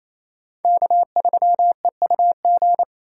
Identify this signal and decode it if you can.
Morse code, heard as a keyed tone, about 28 wpm, 700 Hz. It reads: K3EUG
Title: USA Call Signs - 2x Character Spacing - 28wpm